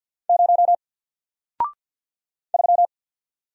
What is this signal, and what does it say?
Morse code, heard as a keyed tone, about 50 wpm, 700 Hz. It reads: 0 3
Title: Single Letter-Number Alt 50wpm